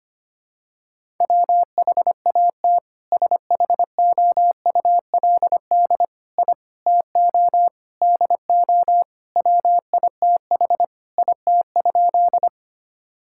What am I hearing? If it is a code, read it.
Morse code, heard as a keyed tone, about 25 wpm, 700 Hz. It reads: WHAT SHOULD I TO DO WITH IT?
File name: practice / 25wpm / Sentences with 4-Word Phrases - 25wpm